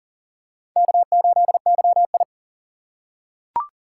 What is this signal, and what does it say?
Morse code, heard as a keyed tone, about 39 wpm, 700 Hz. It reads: K8YI
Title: USA Call Signs - 6x Character Spacing - 40wpm